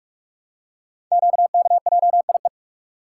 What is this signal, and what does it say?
Morse code, heard as a keyed tone, about 44 wpm, 690 Hz. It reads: QKJIE
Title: Code Groups - 5 Letters - 45wpm